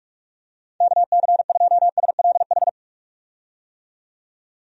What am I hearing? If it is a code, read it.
Morse code, heard as a keyed tone, about 46 wpm, 700 Hz. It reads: KC2SLH
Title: USA Call Signs - 6 Characters - 45wpm